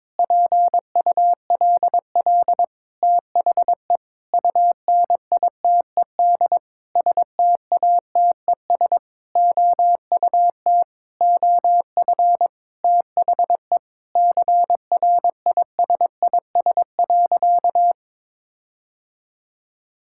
Morse code, heard as a keyed tone, about 22 wpm, 700 Hz. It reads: PULL THE UNITED STATES OUT OF THE CRISIS.
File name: practice / 22wpm / Sentences with 8-Word Phrases - 22wpm